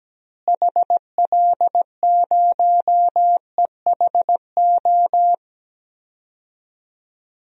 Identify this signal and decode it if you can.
Morse code, heard as a keyed tone, about 17 wpm, 700 Hz. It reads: HL0EHO
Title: DXCC-100 Call Signs - 6 Characters - Repeated 3x - 17wpm